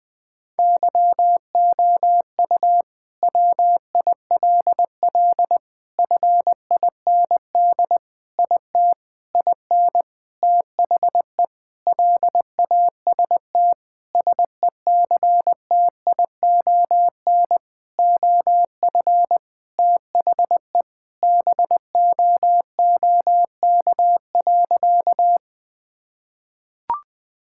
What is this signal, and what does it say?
Morse code, heard as a keyed tone, about 20 wpm, 700 Hz. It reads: YOU WILL FIND IT IN THE LAST SECTION OF THE BOOK.
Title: Sentences from Top 1000 Words - Review Part 5 - 20wpm